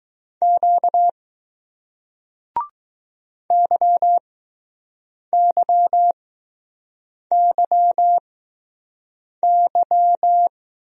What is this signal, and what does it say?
Morse code, heard as a keyed tone, about 20 wpm, spaced as for 6 wpm, 700 Hz. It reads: Q YYYY